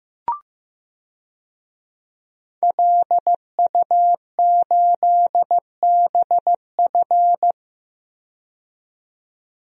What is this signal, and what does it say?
Morse code, heard as a keyed tone, about 15 wpm, 700 Hz. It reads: LU8BF
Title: DXCC-100 Call Signs - 5 Characters - 15wpm